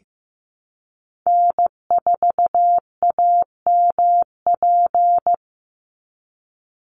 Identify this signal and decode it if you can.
Morse code, heard as a keyed tone, about 15 wpm, 700 Hz. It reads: N4AMP
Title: USA Call Signs 15wpm